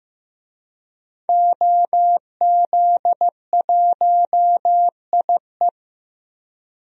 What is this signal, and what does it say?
Morse code, heard as a keyed tone, about 15 wpm, 700 Hz. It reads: OZ1IE
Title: DXCC-100 Call Signs - 5 Characters - 15wpm